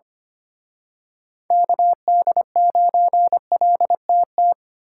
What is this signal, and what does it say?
Morse code, heard as a keyed tone, about 29 wpm, 700 Hz. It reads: KD9LTT
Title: Speed Racing - USA 2x3 Call Signs 25wpm